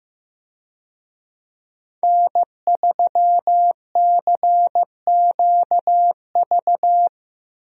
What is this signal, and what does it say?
Morse code, heard as a keyed tone, about 16 wpm, 700 Hz. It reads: N3CQV
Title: USA Call Signs - 5 Characters - Repeated 2x - 15wpm